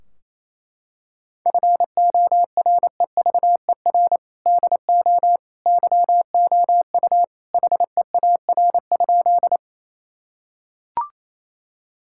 Morse code, heard as a keyed tone, about 28 wpm, 700 Hz. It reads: FOREVER DO YOU HEAR?
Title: Sentences from Top 1600 Words 28wpm